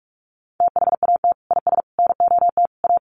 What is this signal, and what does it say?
Morse code, heard as a keyed tone, about 44 wpm, 700 Hz. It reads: THAT IS NOT A
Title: Sentences from Top 600 Words 45wpm